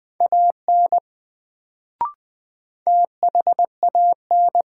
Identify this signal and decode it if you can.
Morse code, heard as a keyed tone, about 20 wpm, 700 Hz. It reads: AN THAN
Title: Sets of 2 Words - New Word Drill - 20wpm